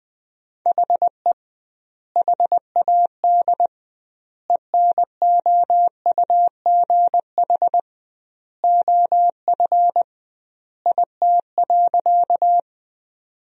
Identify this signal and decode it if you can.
Morse code, heard as a keyed tone, about 20 wpm, 700 Hz. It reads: HE HAD ENOUGH OF IT.
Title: 2x Word-Spacing - Sentences from Top 400 Words - 20wpm